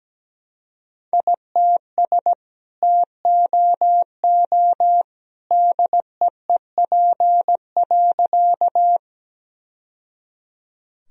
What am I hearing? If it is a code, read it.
Morse code, heard as a keyed tone, about 17 wpm, 700 Hz. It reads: ITS TOO DEEP.
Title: Sentences from Top 1000 Words - Review Part 3 - 17wpm